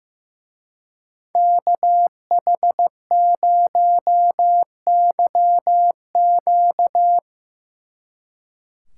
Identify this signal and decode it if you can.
Morse code, heard as a keyed tone, about 17 wpm, 700 Hz. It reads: KH0YQ
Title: Speed Racing - USA 2x2 Call Signs 15wpm